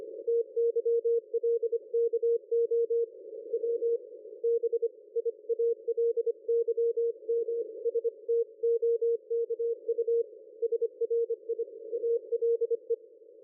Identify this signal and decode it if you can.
Morse code, heard as a keyed tone, about 25 wpm, 460 Hz. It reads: TYLKO W BIALYMSTOKU SRI ALE
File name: dlazaawansowanych / nagraniazpasma / ExpertSDR2 line out 23.01.2020 14.12.00